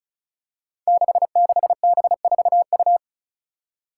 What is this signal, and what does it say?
Morse code, heard as a keyed tone, about 34 wpm, 700 Hz. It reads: 66B4U
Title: Five Challenging Letter-Number Combinations 35wpm